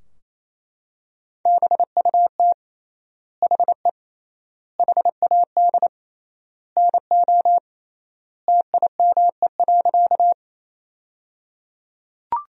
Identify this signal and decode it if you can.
Morse code, heard as a keyed tone, about 28 wpm, 700 Hz. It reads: BUT HE HAD NO TIME.
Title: 3x Word-Spacing - Sentences from Top 100 Words - 28wpm